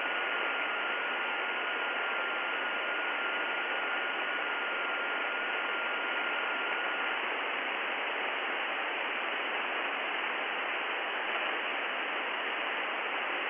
FT8.
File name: contests / 2019-june=vhf-k5tr- -0101 0-20